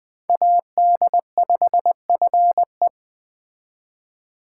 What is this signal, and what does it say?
Morse code, heard as a keyed tone, about 20 wpm, 700 Hz. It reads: AD5FE
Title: Speed Racing - USA 2x2 Call Signs 17wpm